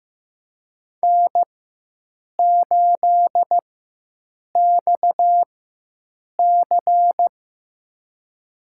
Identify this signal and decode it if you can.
Morse code, heard as a keyed tone, about 15 wpm, spaced as for 7 wpm, 700 Hz. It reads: N8XC
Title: USA Call Signs - 4 Characters - 4x Character Spacing - 15wpm